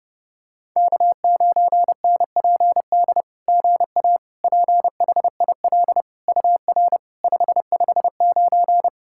Morse code, heard as a keyed tone, about 30 wpm, 700 Hz. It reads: K9NPD GA PHIL UR 559
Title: SOTA QSO Practice 30wpm